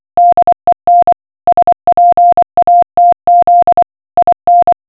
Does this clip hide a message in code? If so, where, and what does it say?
Yes: Morse code from 0.2 to 4.7 s, heard as a keyed tone, about 24 wpm, 700 Hz. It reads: DEN SPATZ IN